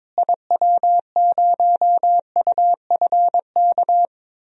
Morse code, heard as a keyed tone, about 22 wpm, 700 Hz. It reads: IW0UFK
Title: DXCC-100 Call Signs - 6 Characters - Repeated 2x - 22wpm